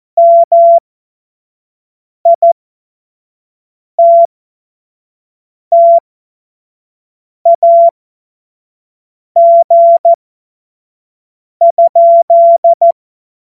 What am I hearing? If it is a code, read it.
Morse code, heard as a keyed tone, about 14 wpm, spaced as for 5 wpm, 680 Hz. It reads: MITTAG?
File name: Morsekurs / Witze / Witz 19 Dezember 2023 25 F70